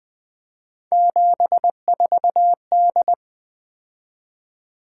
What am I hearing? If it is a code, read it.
Morse code, heard as a keyed tone, about 21 wpm, 700 Hz. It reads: 74D
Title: Three Letter-Number Combinations - 20wpm